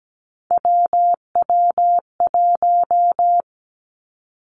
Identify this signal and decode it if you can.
Morse code, heard as a keyed tone, about 17 wpm, 700 Hz. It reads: WW1
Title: Three Challenging Letter-Number Combinations 17wpm